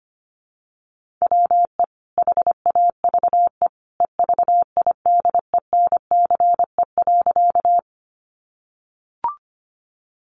Morse code, heard as a keyed tone, about 25 wpm, 700 Hz. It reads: WE HAVE EVIDENCE.
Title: Sentences from Top 1300 Words 25wpm